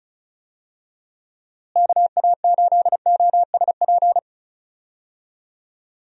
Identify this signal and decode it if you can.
Morse code, heard as a keyed tone, about 36 wpm, 700 Hz. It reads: KA8OSP
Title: USA Call Signs - 6 Characters - Repeated 2x - 35wpm